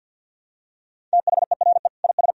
Morse code, heard as a keyed tone, about 52 wpm, 700 Hz. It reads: THERE IS
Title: Sentences from Top 1000 Words - Review Part 1 - 50wpm